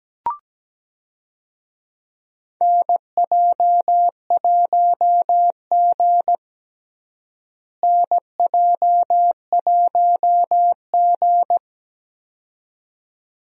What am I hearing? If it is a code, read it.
Morse code, heard as a keyed tone, about 17 wpm, 700 Hz. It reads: NJ1G NJ1G
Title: USA Call Signs - 4 Characters - Repeated 2x - 17wpm